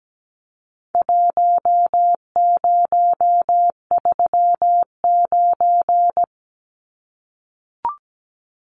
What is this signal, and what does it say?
Morse code, heard as a keyed tone, about 17 wpm, 700 Hz. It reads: 1039
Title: Numbers - Contest Style - 4-Digit 17wpm